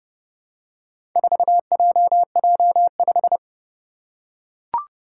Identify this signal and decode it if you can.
Morse code, heard as a keyed tone, about 30 wpm, 700 Hz. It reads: 4JJ5
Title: Four Challenging Letter-Number Combinations 30wpm